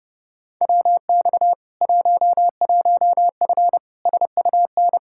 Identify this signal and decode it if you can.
Morse code, heard as a keyed tone, about 30 wpm, 700 Hz. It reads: WX 11F SUN
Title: Practice QSO - Exchange 2 - Info - 30wpm